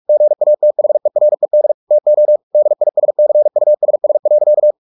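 Morse code, heard as a keyed tone, about 44 wpm, 600 Hz. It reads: GATHERED TO DISCUSS.
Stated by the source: Morse practice at 45 wpm, the novel Never by Ken Follett in English, from CW med Gnister.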